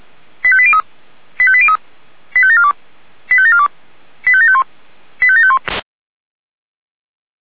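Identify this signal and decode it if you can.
Five-tone selective call (ZVEI), heard as quick runs of five tones, tones of about 70 ms. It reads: 85803 85803 85742 85742 85751 85751
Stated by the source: ZVEI five-tone selective call 85730 received on an SDR.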